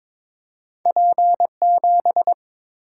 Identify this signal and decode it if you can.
Morse code, heard as a keyed tone, about 22 wpm, 700 Hz. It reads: P7
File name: practice / 22wpm / Two Letter-Number Combinations - 22wpm